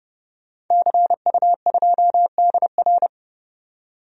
Morse code, heard as a keyed tone, about 31 wpm, 700 Hz. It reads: CU2DR
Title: DXCC-100 Call Signs - 5 Characters - 30wpm